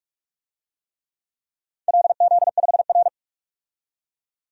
Morse code, heard as a keyed tone, about 44 wpm, 700 Hz. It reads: LZ5R
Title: DXCC-100 Call Signs - 4 Characters - 45wpm